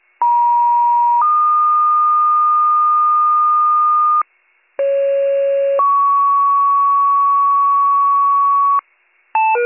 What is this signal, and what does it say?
Two-tone page, heard as a one-second tone, then a three-second tone, about 950 then 1220 Hz, about 550 then 1120 Hz. It is heard from 0.2 to 8.8 s.